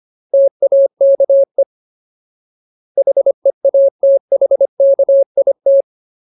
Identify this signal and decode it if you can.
Morse code, heard as a keyed tone, about 25 wpm, 550 Hz. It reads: TAKE HEATHKIT
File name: Ultimate Word List / 25WPM Ultimate